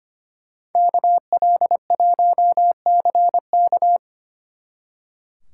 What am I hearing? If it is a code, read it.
Morse code, heard as a keyed tone, about 31 wpm, 700 Hz. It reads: KL1CK
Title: Speed Racing - USA 2x2 Call Signs 25wpm